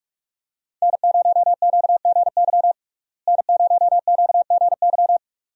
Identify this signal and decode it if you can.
Morse code, heard as a keyed tone, about 46 wpm, 700 Hz. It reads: N0QGY N0QGY
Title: USA Call Signs - 5 Characters - Repeated 2x - 45wpm